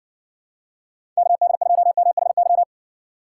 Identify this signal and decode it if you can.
Morse code, heard as a keyed tone, about 60 wpm, 700 Hz. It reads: KD2MHY